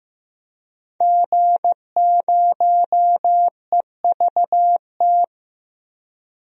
Morse code, heard as a keyed tone, about 15 wpm, 700 Hz. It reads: G0EVT
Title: DXCC-100 Call Signs - 5 Characters - Repeated 3x - 15wpm